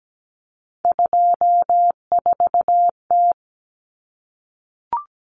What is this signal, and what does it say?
Morse code, heard as a keyed tone, about 17 wpm, 700 Hz. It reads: 24T
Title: Common Cut Numbers - 3-Digit 17wpm